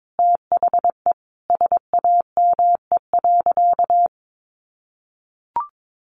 Morse code, heard as a keyed tone, about 22 wpm, 700 Hz. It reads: THE SAME.
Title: Sentences from Top 1300 Words 22wpm